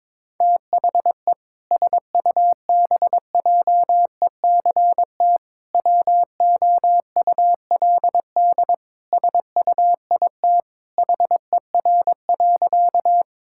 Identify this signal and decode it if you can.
Morse code, heard as a keyed tone, about 22 wpm, 700 Hz. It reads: THE SUBJECT WOULD SUIT HER.
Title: Sentences from Top 1600 Words 22wpm